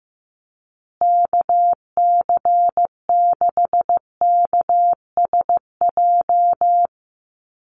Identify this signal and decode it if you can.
Morse code, heard as a keyed tone, about 15 wpm, 700 Hz. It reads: KC6KSJ
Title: USA Call Signs Repeated 3x 15wpm